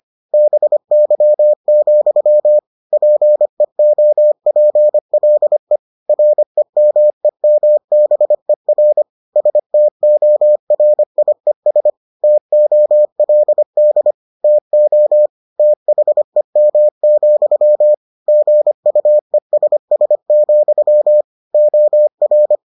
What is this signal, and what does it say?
Morse code, heard as a keyed tone, about 25 wpm, 600 Hz. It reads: BY, PEOPLE REMEMBER STORIES TOLD TO THEM, GUESS, OR